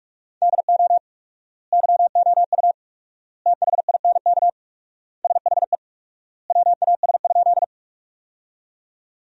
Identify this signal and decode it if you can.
Morse code, heard as a keyed tone, about 42 wpm, 700 Hz. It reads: DO YOU THINK SHE WAS?
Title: 4x Word-Spacing - Sentences from Top 200 Words - 45wpm